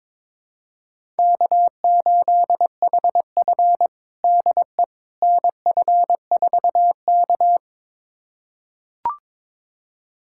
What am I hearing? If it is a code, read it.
Morse code, heard as a keyed tone, about 22 wpm, 700 Hz. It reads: K8HF DE NF4K
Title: USA Callsign DE Callsign - 4 Characters - 22wpm